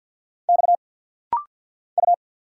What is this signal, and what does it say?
Morse code, heard as a keyed tone, about 50 wpm, 700 Hz. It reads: X U
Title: Single Letter-Number - Rapid-Fire 50wpm